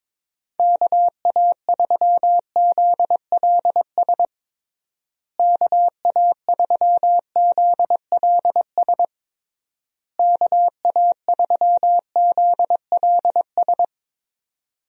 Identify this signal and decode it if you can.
Morse code, heard as a keyed tone, about 21 wpm, 700 Hz. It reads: KA3ZLS KA3ZLS KA3ZLS
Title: USA Call Signs - 6 Characters - Repeated 3x - 22wpm